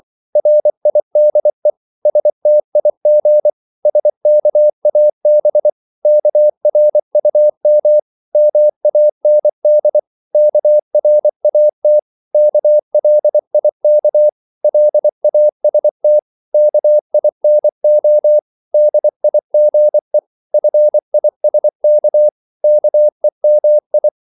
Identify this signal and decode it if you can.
Morse code, heard as a keyed tone, about 24 wpm, 600 Hz. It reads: RIDE STIG SKAB KRUM MAND KRAT KLIK LAST KINO DIGE FISK KEMI
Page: Korte danske ord 24wpm